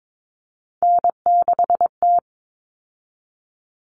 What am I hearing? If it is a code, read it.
Morse code, heard as a keyed tone, about 22 wpm, 700 Hz. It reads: N6T
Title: All Cut Numbers - 3-Digit 22wpm